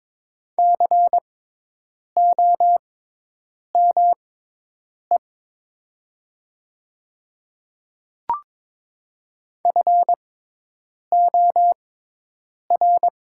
Morse code, heard as a keyed tone, about 22 wpm, spaced as for 8 wpm, 700 Hz. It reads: COME FOR
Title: 6x Character Spacing - Top 100 Words - 22wpm